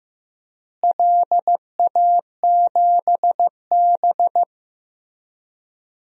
Morse code, heard as a keyed tone, about 15 wpm, 700 Hz. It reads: LA7B
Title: DXCC-100 Call Signs - 4 Characters - Repeated 2x - 15wpm